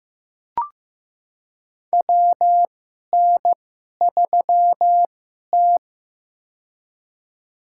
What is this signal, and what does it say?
Morse code, heard as a keyed tone, about 15 wpm, spaced as for 11 wpm, 700 Hz. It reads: WN3T
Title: USA Call Signs - 4 Characters - 2x Character Spacing - 15wpm